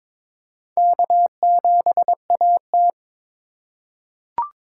Morse code, heard as a keyed tone, about 22 wpm, 700 Hz. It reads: K7AT
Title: USA Call Signs - 4 Characters - 2x Character Spacing - 22wpm